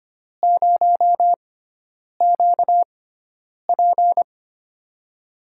Morse code, heard as a keyed tone, about 25 wpm, spaced as for 9 wpm, 700 Hz. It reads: 0QP
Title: USA Call Signs - 5 Characters - 6x Character Spacing - 25wpm